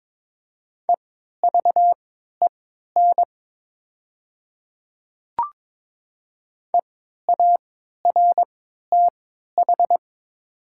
Morse code, heard as a keyed tone, about 22 wpm, spaced as for 12 wpm, 700 Hz. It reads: EVEN EARTH
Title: 3x Character Spacing - Top 300 Words - 22wpm